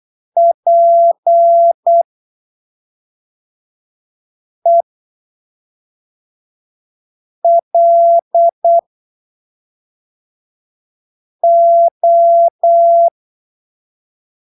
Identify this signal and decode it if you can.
Morse code, heard as a keyed tone, about 8 wpm, spaced as for 3 wpm, 680 Hz. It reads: PELO
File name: arquivos / texto / texto 21 08ppm